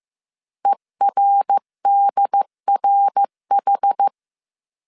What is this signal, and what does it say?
Morse code, heard as a keyed tone, about 15 wpm, 800 Hz. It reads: ERDRH
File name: CW / DET / 1 3 1500C S C 14wpm 2458